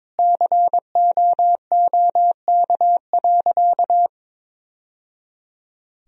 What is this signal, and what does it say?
Morse code, heard as a keyed tone, about 22 wpm, 700 Hz. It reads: COOK.
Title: Sentences from Top 1000 Words - Review Part 4 - 22wpm